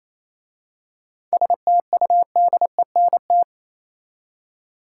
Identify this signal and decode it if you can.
Morse code, heard as a keyed tone, about 27 wpm, 700 Hz. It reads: STUDENT
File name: Easy Words - Speed-Racing with Hints - 22wpm